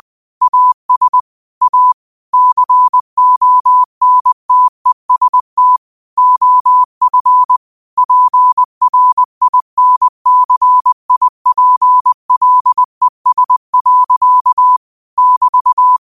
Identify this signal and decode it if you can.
Morse code, heard as a keyed tone, about 20 wpm, 1000 Hz. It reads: AS A CONTEST OF PRINCIPLES. <BT>
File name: qotdfiles / 20wpm